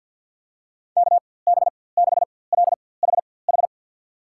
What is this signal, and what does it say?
Morse code, heard as a keyed tone, about 53 wpm, spaced as for 24 wpm, 700 Hz. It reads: KB6LHH
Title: USA Call Signs - 6 Characters - 4x Character Spacing - 50wpm